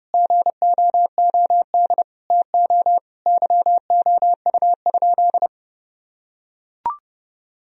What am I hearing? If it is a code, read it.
Morse code, heard as a keyed tone, about 30 wpm, 700 Hz. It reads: GOOD TO YOU?
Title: Sentences from Top 1000 Words - Review Part 4 - 30wpm